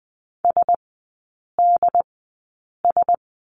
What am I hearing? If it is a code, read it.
Morse code, heard as a keyed tone, about 20 wpm, spaced as for 8 wpm, 700 Hz. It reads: SDS
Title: 114 Lesson 19e - Letter w - Three Characters 20wpm